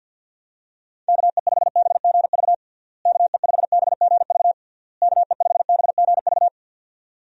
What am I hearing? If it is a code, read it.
Morse code, heard as a keyed tone, about 49 wpm, 700 Hz. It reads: KE5BGV KE5BGV KE5BGV
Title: USA Call Signs - 6 Characters - Repeated 3x - 50wpm